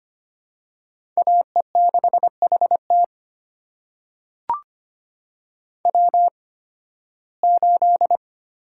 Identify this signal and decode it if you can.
Morse code, heard as a keyed tone, about 25 wpm, spaced as for 16 wpm, 700 Hz. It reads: AE6HT W8
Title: USA Call Signs - 5 Characters - 8x Character Spacing - 25wpm